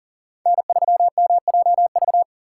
Morse code, heard as a keyed tone, about 37 wpm, 700 Hz. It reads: N3MJV
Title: USA Call Signs - 5 Characters - 3x Character Spacing - 40wpm